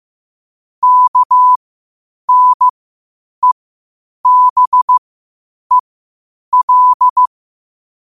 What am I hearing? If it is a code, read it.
Morse code, heard as a keyed tone, about 15 wpm, spaced as for 8 wpm, 1000 Hz. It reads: KNEBEL
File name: qotdfiles / 8wpm